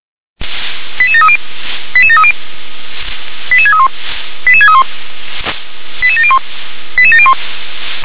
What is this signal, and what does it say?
Five-tone selective call (ZVEI), heard as quick runs of five tones, tones of about 70 ms. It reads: 88730 88730 88631 88631 88701 88701